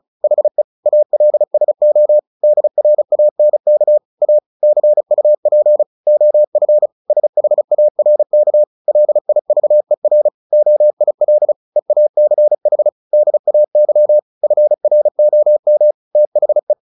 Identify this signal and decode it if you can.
Morse code, heard as a keyed tone, about 34 wpm, 600 Hz. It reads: HE ALSO DRANK A CUP OF SHARK LIVER OIL EACH DAY FROM THE
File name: the old man and the sea / the old man and the sea 1 35wpm